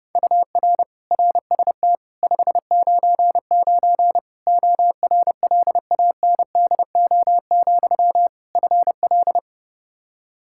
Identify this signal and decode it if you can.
Morse code, heard as a keyed tone, about 30 wpm, 700 Hz. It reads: UR RST 599 ORLANDO, FL
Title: Practice QSO - Exchange 1 - Beginning - 30wpm